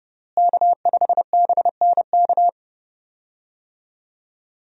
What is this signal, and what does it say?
Morse code, heard as a keyed tone, about 30 wpm, 700 Hz. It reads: K5BNK
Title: USA Call Signs - 5 Characters - Repeated 3x - 30wpm